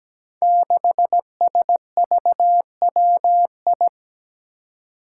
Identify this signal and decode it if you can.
Morse code, heard as a keyed tone, about 17 wpm, 700 Hz. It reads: 6SVWI
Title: Five Challenging Letter-Number Combinations 17wpm